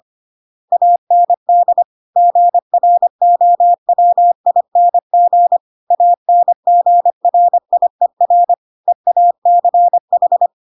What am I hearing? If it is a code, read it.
Morse code, heard as a keyed tone, about 25 wpm, 700 Hz. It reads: AND GROWING ANGRIER EACH
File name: Pinocchio / Pinocchio ch2 25 25 ik1hge